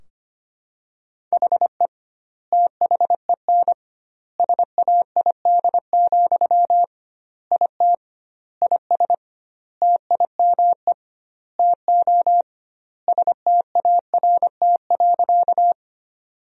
Morse code, heard as a keyed tone, about 25 wpm, 700 Hz. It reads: HE THEN SAID, IT IS TIME TO START.
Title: 2x Word-Spacing - Sentences from Top 300 Words - 25wpm